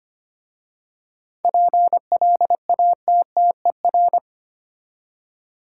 Morse code, heard as a keyed tone, about 25 wpm, 690 Hz. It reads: PLATTER
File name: practice / 25wpm / Easy Words with Hints - 25wpm